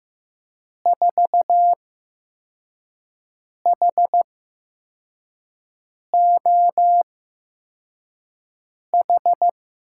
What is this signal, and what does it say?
Morse code, heard as a keyed tone, about 15 wpm, spaced as for 4 wpm, 700 Hz. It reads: 4HOH
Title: USA Call Signs - 6 Characters - 8x Character Spacing - 15wpm